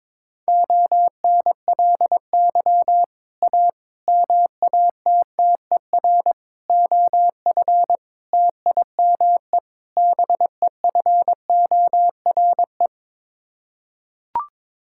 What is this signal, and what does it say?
Morse code, heard as a keyed tone, about 22 wpm, 700 Hz. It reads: ONLY A MATTER OF TIME BEFORE
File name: practice / 22wpm / Sets of 6 Words - 22wpm